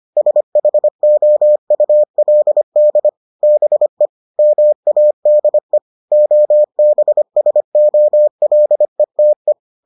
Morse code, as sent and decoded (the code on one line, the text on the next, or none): ... .... --- ..- .-.. -.. / -... / -- .- -.. / --- -... ... --- .-.. - .
SHOULD BE MADE OBSOLETE